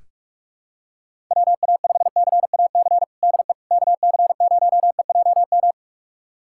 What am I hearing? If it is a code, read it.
Morse code, heard as a keyed tone, about 41 wpm, 690 Hz. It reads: WA5CAC DE KC0EJM
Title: USA Callsign DE Callsign 45wpm